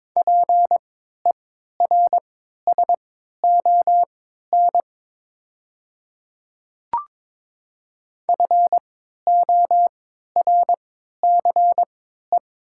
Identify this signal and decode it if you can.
Morse code, heard as a keyed tone, about 22 wpm, spaced as for 13 wpm, 700 Hz. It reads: PERSON FORCE
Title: 3x Character Spacing - Top 500 Words - 22wpm